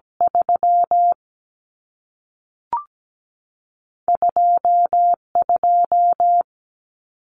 Morse code, heard as a keyed tone, about 17 wpm, spaced as for 3 wpm, 700 Hz. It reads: 32 2
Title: Numbers - 2-Digit 17wpm